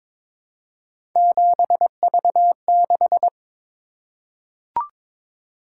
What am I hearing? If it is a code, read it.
Morse code, heard as a keyed tone, about 22 wpm, 700 Hz. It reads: 7V6
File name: practice / 22wpm / Three Letter-Number Combinations - 22wpm